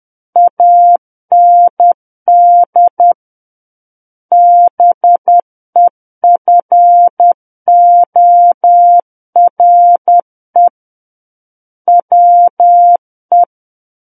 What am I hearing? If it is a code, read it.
Morse code, heard as a keyed tone, about 10 wpm, 700 Hz. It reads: AND BEFORE WE
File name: Book One / 01-Chapter 1 - 10 WPM